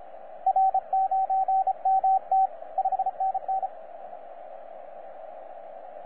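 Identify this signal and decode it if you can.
Morse code, heard as a keyed tone, about 28 wpm, 770 Hz, with radio noise. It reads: R9MT 5NN
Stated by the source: 3D2SE heard on 15 metres CW, a DX World.net recording.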